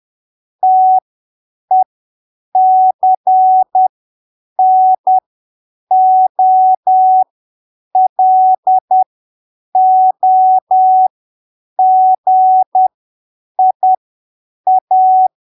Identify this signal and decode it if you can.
Morse code, heard as a keyed tone, about 10 wpm, spaced as for 7 wpm, 750 Hz. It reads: TECNOLOGIA
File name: arquivos / texto / texto 24 10ppm